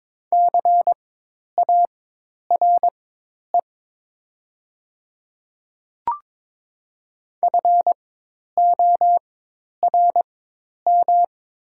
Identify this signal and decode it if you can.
Morse code, heard as a keyed tone, about 22 wpm, spaced as for 10 wpm, 700 Hz. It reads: CARE FORM
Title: 4x Character Spacing - Top 500 Words - 22wpm